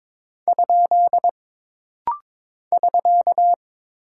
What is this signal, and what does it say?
Morse code, heard as a keyed tone, about 22 wpm, 700 Hz. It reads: ? <SK>
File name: practice / 22wpm / Single Punctuation-Prosign - Rapid-Fire - 22wpm